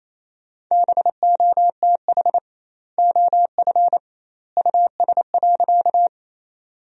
Morse code, heard as a keyed tone, about 28 wpm, 700 Hz. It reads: BOTH OF US.
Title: 2x Word-Spacing - Sentences from Top 300 Words - 28wpm